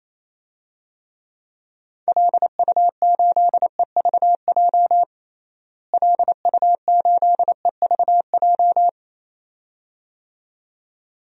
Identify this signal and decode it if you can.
Morse code, heard as a keyed tone, about 28 wpm, 700 Hz. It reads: LU8EVJ LU8EVJ
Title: DXCC-100 Call Signs - 6 Characters - Repeated 2x - 28wpm